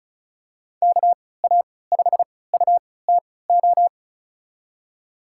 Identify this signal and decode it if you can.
Morse code, heard as a keyed tone, about 34 wpm, spaced as for 20 wpm, 700 Hz. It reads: KA5UTO
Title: USA Call Signs - 6 Characters - 3x Character Spacing - 35wpm